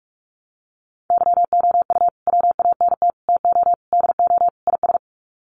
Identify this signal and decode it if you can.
Morse code, heard as a keyed tone, about 44 wpm, 690 Hz. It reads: YOU WANT TO DO IS
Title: Sets of 5 Words Encore2 45wpm